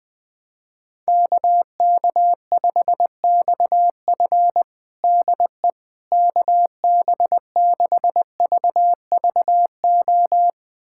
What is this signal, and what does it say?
Morse code, heard as a keyed tone, about 20 wpm, 700 Hz. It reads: KK5XF DE KB6VVO
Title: USA Callsign DE Callsign 20wpm